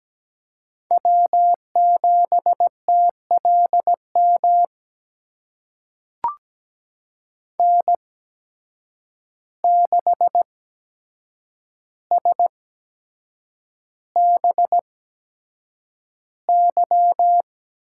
Morse code, heard as a keyed tone, about 17 wpm, spaced as for 7 wpm, 700 Hz. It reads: W7TLM N6SBY
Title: USA Call Signs - 8x Character Spacing - 17wpm